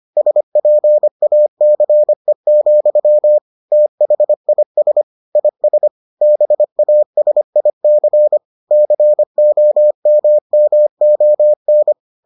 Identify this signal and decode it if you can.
Morse code, heard as a keyed tone, about 25 wpm, 600 Hz. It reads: SPACE, THIS IS BASIC COMMON